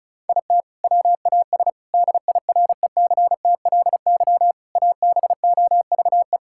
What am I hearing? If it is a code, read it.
Morse code, heard as a keyed tone, about 36 wpm, 700 Hz. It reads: IT WAS DIRECTLY ABOVE
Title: Sentences from Top 1000 Words - Review Part 5 - 35wpm